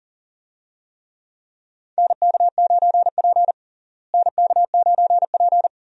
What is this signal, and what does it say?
Morse code, heard as a keyed tone, about 40 wpm, 700 Hz. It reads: NK9P NK9P
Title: USA Call Signs - 4 Characters - Repeated 2x - 40wpm